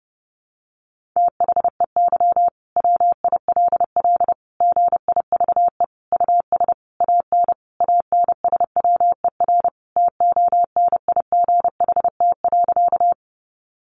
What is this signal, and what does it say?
Morse code, heard as a keyed tone, about 30 wpm, 700 Hz. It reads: THEY WILL GIVE US AN ANSWER TONIGHT.
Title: Sentences from Top 600 Words 30wpm